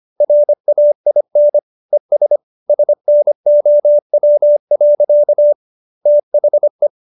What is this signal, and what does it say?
Morse code, heard as a keyed tone, about 25 wpm, 590 Hz. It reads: RAIN ES SNOW. THE